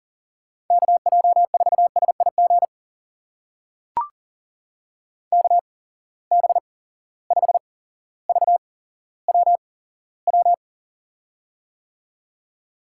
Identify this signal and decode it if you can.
Morse code, heard as a keyed tone, about 38 wpm, spaced as for 17 wpm, 700 Hz. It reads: KJ4SIG KB5VWW
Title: USA Call Signs - 6 Characters - 8x Character Spacing - 40wpm